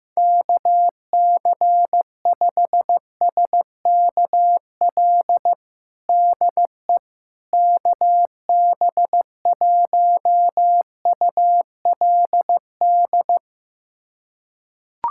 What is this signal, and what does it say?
Morse code, heard as a keyed tone, about 15 wpm, 700 Hz. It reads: KC5SKL DE KB1ULD
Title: USA Callsign DE Callsign - 6 Characters - 15wpm